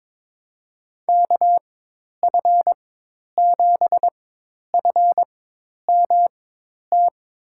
Morse code, heard as a keyed tone, about 21 wpm, spaced as for 10 wpm, 700 Hz. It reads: KF7FMT
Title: USA Call Signs - 6 Characters - 4x Character Spacing - 22wpm